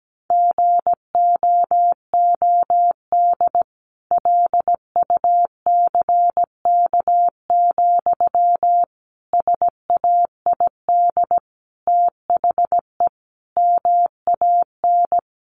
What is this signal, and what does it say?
Morse code, heard as a keyed tone, about 17 wpm, 700 Hz. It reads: GOOD LUCK, SAID THE MAN
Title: A Sound Of Thunder 17wpm